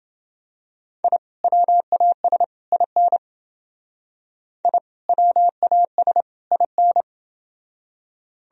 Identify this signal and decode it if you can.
Morse code, heard as a keyed tone, about 28 wpm, 700 Hz. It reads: I WAS IN I WAS IN